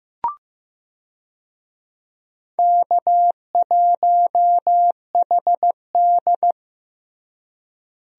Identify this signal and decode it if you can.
Morse code, heard as a keyed tone, about 15 wpm, 700 Hz. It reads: K1HD
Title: USA Call Signs - 4 Characters - Repeated 3x - 15wpm